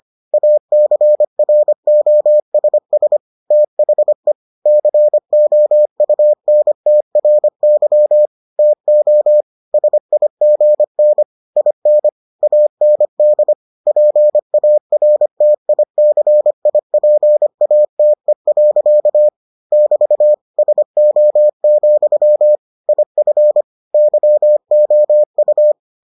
Morse code, heard as a keyed tone, about 25 wpm, 600 Hz. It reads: ACROSS THE COUNTRY TO SIGN IN AND PARTICIPATE. <BT> SO, IF YOU